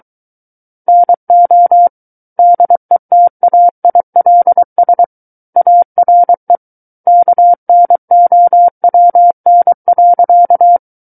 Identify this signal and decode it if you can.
Morse code, heard as a keyed tone, about 23 wpm, 700 Hz. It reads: NO DETAILS ARE KNOWN.
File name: Book One / 14-Chapter 14 - 23 WPM